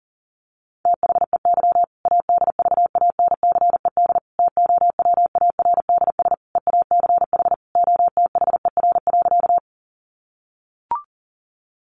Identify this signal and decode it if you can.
Morse code, heard as a keyed tone, about 40 wpm, 700 Hz. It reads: THEY ADVANCED TOWARDS EACH OTHER.
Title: Sentences from Top 1200 Words 40wpm